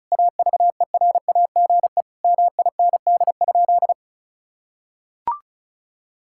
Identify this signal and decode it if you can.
Morse code, heard as a keyed tone, about 36 wpm, 700 Hz. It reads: AVERAGE MIND?
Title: Sentences from Top 1000 Words - Review Part 6 - 35wpm